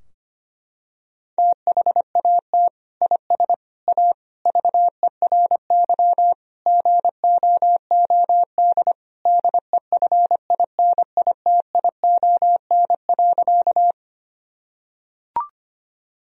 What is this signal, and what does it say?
Morse code, heard as a keyed tone, about 25 wpm, 700 Hz. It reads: THAT IS A VERY GOOD DEFINITION.
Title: Sentences from Top 1000 Words - Review Part 4 - 25wpm